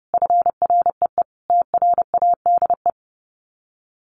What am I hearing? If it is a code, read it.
Morse code, heard as a keyed tone, about 30 wpm, 700 Hz. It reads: FREE TRADE
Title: Sets of 5 Words Encore 30wpm